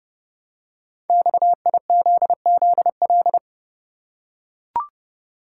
Morse code, heard as a keyed tone, about 29 wpm, 700 Hz. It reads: XIZZL
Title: Code Groups - 5 Letters - 30wpm